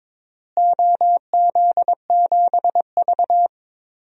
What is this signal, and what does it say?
Morse code, heard as a keyed tone, about 22 wpm, 700 Hz. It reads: OZ7V
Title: DXCC-100 Call Signs - 4 Characters - Repeated 3x - 22wpm